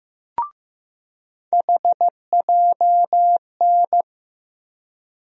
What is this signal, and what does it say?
Morse code, heard as a keyed tone, about 15 wpm, 700 Hz. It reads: HJN
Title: USA Call Sign Postfixes - 3 Letters 15wpm